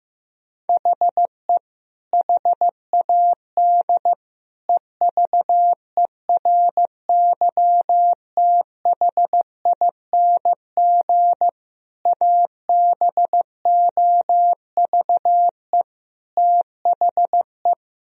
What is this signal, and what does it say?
Morse code, heard as a keyed tone, about 15 wpm, 700 Hz. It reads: HE HAD EVERYTHING ABOVE THE
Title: Sentences from Top 1000 Words - Review Part 3 - 15wpm